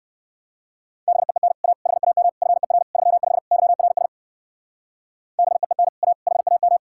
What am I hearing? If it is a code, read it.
Morse code, heard as a keyed tone, about 57 wpm, 700 Hz. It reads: BEEN A HAM FER 35 YRS BEEN A HAM